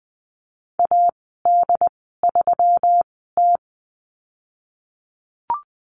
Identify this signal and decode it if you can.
Morse code, heard as a keyed tone, about 20 wpm, spaced as for 15 wpm, 700 Hz. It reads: AD3T
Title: 213 Lesson 34g - Number 8 - Call Signs 20wpm